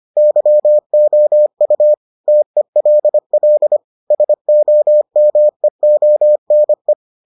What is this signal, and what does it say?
Morse code, heard as a keyed tone, about 25 wpm, 600 Hz. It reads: YOU TELL SOMEONE